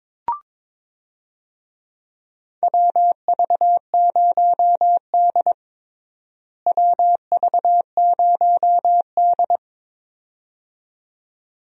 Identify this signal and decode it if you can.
Morse code, heard as a keyed tone, about 22 wpm, 700 Hz. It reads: WV0D WV0D
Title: USA Call Signs - 4 Characters - Repeated 2x - 22wpm